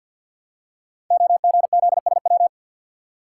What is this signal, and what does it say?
Morse code, heard as a keyed tone, about 50 wpm, 700 Hz. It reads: OGZSW
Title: Code Groups - 5 Letter:Numbers - 50wpm